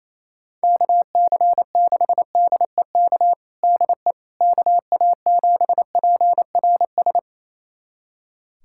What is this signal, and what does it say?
Morse code, heard as a keyed tone, about 28 wpm, 700 Hz. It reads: KC6DEK DE KA7PRS
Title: USA Callsign DE Callsign - 6 Characters - 28wpm